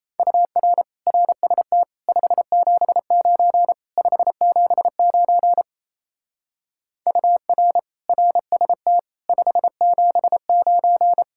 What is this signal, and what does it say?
Morse code, heard as a keyed tone, about 34 wpm, 700 Hz. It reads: UR RST 579 579 UR RST 579
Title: Speed-Racing Common QSO Phrases - 28wpm